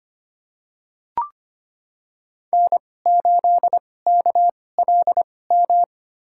Morse code, heard as a keyed tone, about 25 wpm, spaced as for 18 wpm, 700 Hz. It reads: N8KLM
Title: USA Call Signs - 5 Characters - 2x Character Spacing - 25wpm